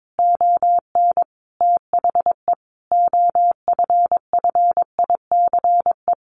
Morse code, heard as a keyed tone, about 22 wpm, 700 Hz. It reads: ON THE OFFICE
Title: A Sound Of Thunder 22wpm